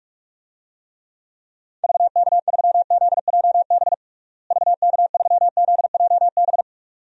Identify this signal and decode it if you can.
Morse code, heard as a keyed tone, about 46 wpm, 700 Hz. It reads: VK3ZJB VK3ZJB
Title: DXCC-100 Call Signs - 6 Characters - Repeated 2x - 45wpm